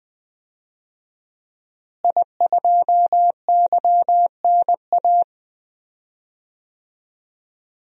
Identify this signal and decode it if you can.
Morse code, heard as a keyed tone, about 20 wpm, 700 Hz. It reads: I2YNA
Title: DXCC-100 Call Signs - 5 Characters - Repeated 2x - 20wpm